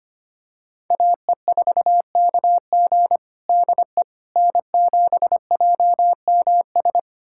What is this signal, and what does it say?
Morse code, heard as a keyed tone, about 25 wpm, 700 Hz. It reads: AE4KG DE N7JMS
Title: USA Callsign DE Callsign - 5 Characters - 25wpm